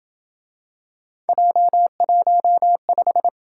Morse code, heard as a keyed tone, about 28 wpm, 700 Hz. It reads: J15